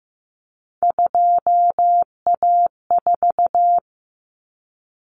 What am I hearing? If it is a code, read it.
Morse code, heard as a keyed tone, about 15 wpm, 700 Hz. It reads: 2A4
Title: Common Cut Numbers - 3-Digit 15wpm